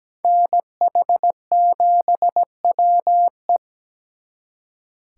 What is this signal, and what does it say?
Morse code, heard as a keyed tone, about 20 wpm, 700 Hz. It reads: NH7WE
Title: Speed Racing - USA 2x2 Call Signs 17wpm